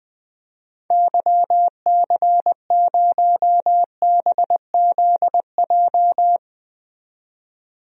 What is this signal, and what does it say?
Morse code, heard as a keyed tone, about 20 wpm, 700 Hz. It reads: YC0BZJ
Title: DXCC-100 Call Signs - 6 Characters - 20wpm